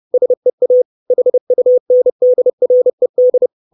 Morse code, heard as a keyed tone, about 30 wpm, 500 Hz. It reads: SEA HUNDRED
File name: 632 Words 30WPM